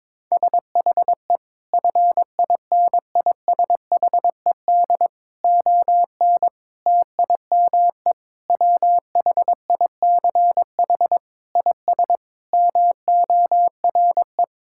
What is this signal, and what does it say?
Morse code, heard as a keyed tone, about 21 wpm, 700 Hz. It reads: SHE FINISHED ON TIME WHICH IS MORE
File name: practice / 22wpm / Sentences with 8-Word Phrases - 22wpm